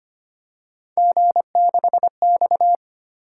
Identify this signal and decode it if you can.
Morse code, heard as a keyed tone, about 25 wpm, 700 Hz. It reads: G6X
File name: Three Letter-Number Combinations - 25wpm